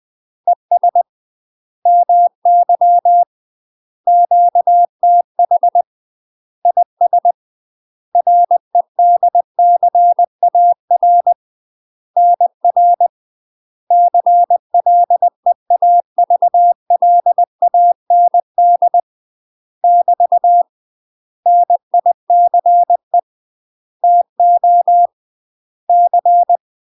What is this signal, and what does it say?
Morse code, heard as a keyed tone, about 20 wpm, 700 Hz. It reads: ES MY QTH IS REDCAR NR CLEAVLAND <BT> NICE TO C
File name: UK MASTER QSOs TRAINING / 20WPM / UK MASTER QSOs 0057